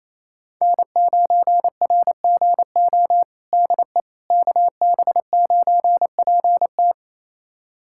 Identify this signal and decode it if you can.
Morse code, heard as a keyed tone, about 28 wpm, 700 Hz. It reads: N9RGO DE KB9PT
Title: USA Callsign DE Callsign - 5 Characters - 28wpm